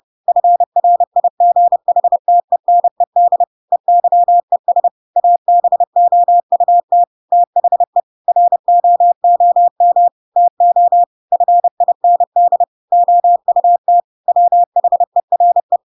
Morse code, heard as a keyed tone, about 30 wpm, 700 Hz. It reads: FRIGHTENED EYES ABOUT THE ROOM TO FIND OUT WHERE
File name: Pinocchio / Pinocchio ch1 30 30 ik1hge